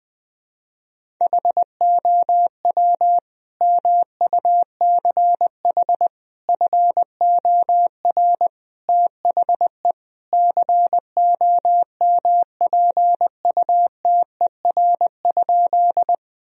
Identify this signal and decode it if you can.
Morse code, heard as a keyed tone, about 21 wpm, 700 Hz. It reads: HOW MUCH FOR THE COMPUTER?
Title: Sentences from Top 1000 Words - Review Part 3 - 20wpm